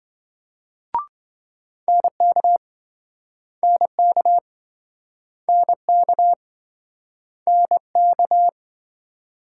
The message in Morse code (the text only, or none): NK NK NK NK